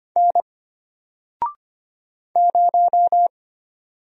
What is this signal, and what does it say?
Morse code, heard as a keyed tone, about 25 wpm, 700 Hz. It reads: N 0
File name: Single-Letter-Number-25wpm